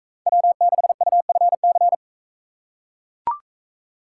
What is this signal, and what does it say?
Morse code, heard as a keyed tone, about 41 wpm, 700 Hz. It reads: W6UFC